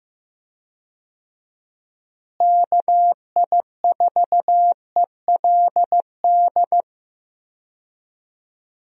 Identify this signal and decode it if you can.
Morse code, heard as a keyed tone, about 15 wpm, 700 Hz. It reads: KI4ELD
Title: USA Call Signs - 6 Characters - Repeated 3x - 15wpm